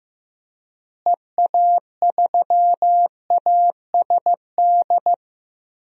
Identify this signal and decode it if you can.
Morse code, heard as a keyed tone, about 15 wpm, 700 Hz. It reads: EA3ASD
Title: DXCC-100 Call Signs - 6 Characters - Repeated 3x - 15wpm